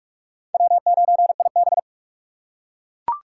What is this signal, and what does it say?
Morse code, heard as a keyed tone, about 44 wpm, 700 Hz. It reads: W9IB